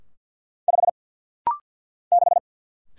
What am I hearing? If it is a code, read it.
Morse code, heard as a keyed tone, about 48 wpm, 700 Hz. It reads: 5 6
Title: Single Letter-Number - Rapid-Fire 50wpm